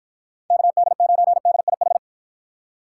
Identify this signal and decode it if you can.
Morse code, heard as a keyed tone, about 53 wpm, 700 Hz. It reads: KD9DIH